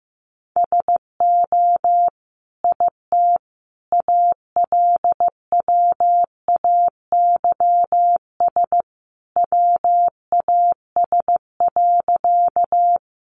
Morse code, heard as a keyed tone, about 15 wpm, 700 Hz. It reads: SO IT ALWAYS WAS.
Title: Sentences from Top 300 Words 15wpm